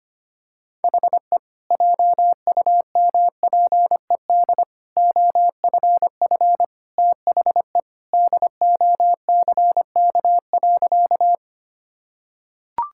Morse code, heard as a keyed tone, about 25 wpm, 700 Hz. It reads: HE JUMPED OFF THE DOCK.
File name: practice / 25wpm / Sentences with 2-Word Phrases - 25wpm